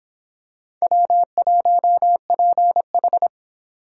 Morse code, heard as a keyed tone, about 23 wpm, 700 Hz. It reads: W1PH